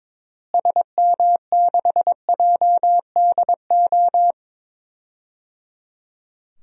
Morse code, heard as a keyed tone, about 22 wpm, 700 Hz. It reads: SM6JDO
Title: DXCC-100 Call Signs - 6 Characters - Repeated 3x - 22wpm